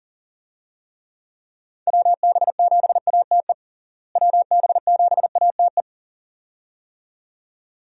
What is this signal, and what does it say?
Morse code, heard as a keyed tone, about 40 wpm, 700 Hz. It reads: WB7ATE WB7ATE
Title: USA Call Signs - 6 Characters - Repeated 2x - 40wpm